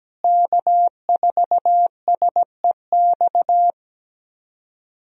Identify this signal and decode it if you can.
Morse code, heard as a keyed tone, about 17 wpm, 700 Hz. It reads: K4SEX
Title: USA Call Signs - 5 Characters - Repeated 3x - 17wpm